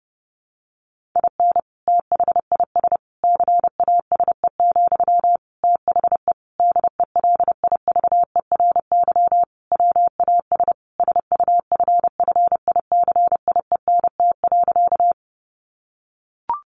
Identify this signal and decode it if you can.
Morse code, heard as a keyed tone, about 30 wpm, 700 Hz. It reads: IN THIS CASE, THE DELIVERY WAS SUFFICIENT.
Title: Sentences from Top 1500 Words 30wpm